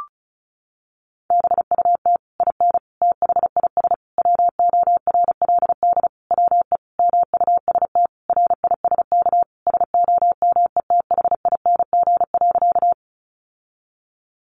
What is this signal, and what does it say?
Morse code, heard as a keyed tone, about 34 wpm, 700 Hz. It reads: BUT IN THIS WORLD WE MUST RISK SOMETHING.
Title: Sentences from Top 600 Words 35wpm